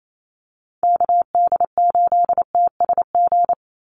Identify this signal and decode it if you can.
Morse code, heard as a keyed tone, about 28 wpm, 700 Hz. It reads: KD8TSG
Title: USA Call Signs 28wpm